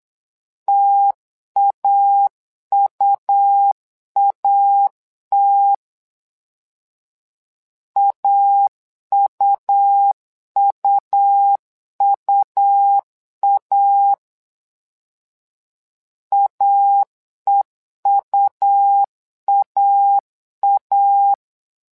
Morse code, heard as a keyed tone, about 8 wpm, 800 Hz. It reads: TAUAT AUUUA AEUAA
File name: LEZIONI / 2 UHDN / 2 U 8wpm 2028